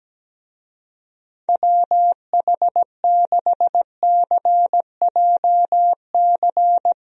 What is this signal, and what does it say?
Morse code, heard as a keyed tone, about 17 wpm, 700 Hz. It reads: WH6CJC
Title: USA Call Signs - 6 Characters - Repeated 3x - 17wpm